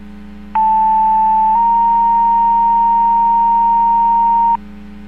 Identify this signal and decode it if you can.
Two-tone page, heard as a one-second tone, then a three-second tone, about 880 then 930 Hz. It is heard from 0.5 to 4.6 s.